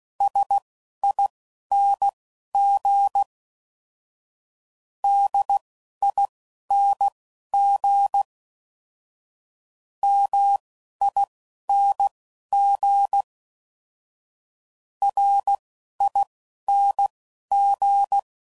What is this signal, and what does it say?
Morse code, heard as a keyed tone, about 16 wpm, spaced as for 11 wpm, 800 Hz. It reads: SING DING MING RING
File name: Suffixes / Special INGs / Single INGs